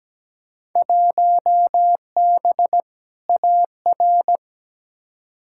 Morse code, heard as a keyed tone, about 17 wpm, 700 Hz. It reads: 1B AR
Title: Field Day Exchanges Encore - 17wpm